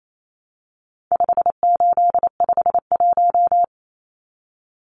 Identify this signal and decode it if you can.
Morse code, heard as a keyed tone, about 28 wpm, 700 Hz. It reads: 5851
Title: Numbers - 4-Digit 28wpm